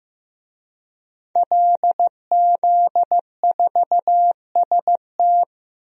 Morse code, heard as a keyed tone, about 15 wpm, 700 Hz. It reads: LZ4ST
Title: DXCC-100 Call Signs - 5 Characters - Repeated 2x - 15wpm